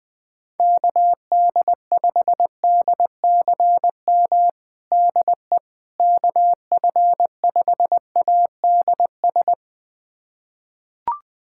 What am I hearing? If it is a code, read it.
Morse code, heard as a keyed tone, about 21 wpm, 700 Hz. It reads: KD5DCM DE KF5ADS
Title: USA Callsign DE Callsign 20wpm